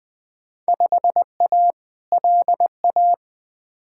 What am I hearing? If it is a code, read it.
Morse code, heard as a keyed tone, about 21 wpm, 700 Hz. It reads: 5A LA
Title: Field Day Exchanges Encore - 20wpm